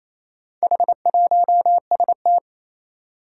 Morse code, heard as a keyed tone, about 27 wpm, 700 Hz. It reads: H1ST
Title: Four Challenging Letter-Number Combinations 28wpm